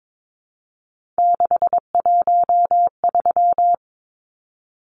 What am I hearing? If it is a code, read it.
Morse code, heard as a keyed tone, about 22 wpm, 700 Hz. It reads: 613
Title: Numbers - 3-Digit 22wpm